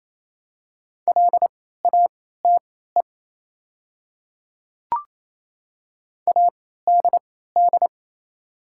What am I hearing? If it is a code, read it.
Morse code, heard as a keyed tone, about 27 wpm, spaced as for 16 wpm, 700 Hz. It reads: LATE ADD
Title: 3x Character Spacing - Top 300 Words - 28wpm